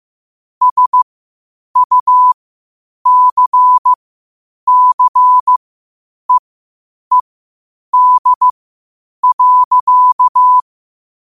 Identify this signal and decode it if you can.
Morse code, heard as a keyed tone, about 15 wpm, spaced as for 8 wpm, 1000 Hz. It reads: SUCCEED.
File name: qotdfiles / 8wpm